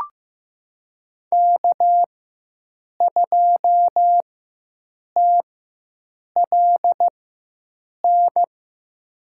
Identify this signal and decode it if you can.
Morse code, heard as a keyed tone, about 15 wpm, spaced as for 7 wpm, 700 Hz. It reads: K2TLN
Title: USA Call Signs - 5 Characters - 4x Character Spacing - 15wpm